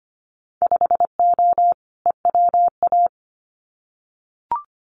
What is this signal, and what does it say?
Morse code, heard as a keyed tone, about 25 wpm, 700 Hz. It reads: 5O EWA
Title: Winter Field Day Exchanges Encore 25wpm